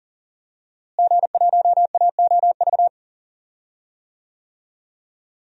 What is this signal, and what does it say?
Morse code, heard as a keyed tone, about 40 wpm, 700 Hz. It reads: G1AOV
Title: DXCC-100 Call Signs - 5 Characters - Repeated 3x - 40wpm